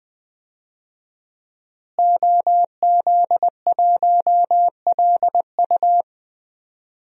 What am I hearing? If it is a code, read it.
Morse code, heard as a keyed tone, about 20 wpm, 700 Hz. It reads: OZ1LU
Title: DXCC-100 Call Signs - 5 Characters - Repeated 2x - 20wpm